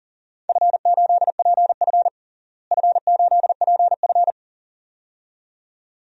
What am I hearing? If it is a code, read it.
Morse code, heard as a keyed tone, about 40 wpm, 700 Hz. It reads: F8PF F8PF
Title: DXCC-100 Call Signs - 4 Characters - Repeated 2x - 40wpm